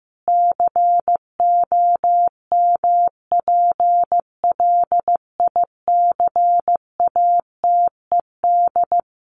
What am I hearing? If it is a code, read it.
Morse code, heard as a keyed tone, about 14 wpm, 700 Hz. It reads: COMPLICATED
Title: Sentences from Top 1200 Words 15wpm